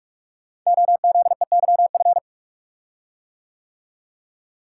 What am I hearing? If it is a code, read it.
Morse code, heard as a keyed tone, about 44 wpm, 700 Hz. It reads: OZEYF
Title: Code Groups - 5 Letter:Numbers - 45wpm